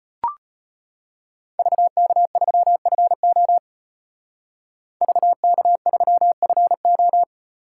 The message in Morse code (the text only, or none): VK3FO VK3FO